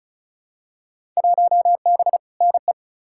Morse code, heard as a keyed tone, about 35 wpm, 700 Hz. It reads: 1B NE
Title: Field Day Exchanges Encore - 35wpm